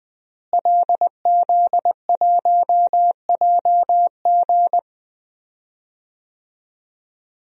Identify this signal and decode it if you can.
Morse code, heard as a keyed tone, about 20 wpm, 700 Hz. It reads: LZ1JG
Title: DXCC-100 Call Signs - 5 Characters - Repeated 2x - 20wpm